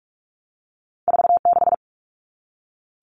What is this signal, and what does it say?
Morse code, heard as a keyed tone, about 46 wpm, 700 Hz. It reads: 46
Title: All Cut Numbers - 2-Digit 45wpm